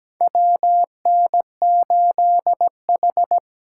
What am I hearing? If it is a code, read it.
Morse code, heard as a keyed tone, about 17 wpm, 700 Hz. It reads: WN8H
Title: USA Call Signs - 4 Characters - 2x Character Spacing - 17wpm